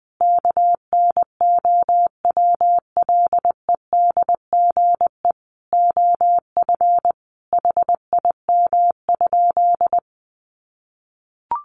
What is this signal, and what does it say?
Morse code, heard as a keyed tone, about 20 wpm, 700 Hz. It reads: KNOWLEDGE OF HIM?
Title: Sentences from Top 600 Words 20wpm